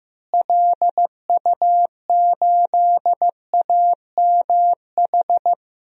Morse code, heard as a keyed tone, about 15 wpm, 700 Hz. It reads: LU8AMH
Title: DXCC-100 Call Signs - 6 Characters - Repeated 2x - 15wpm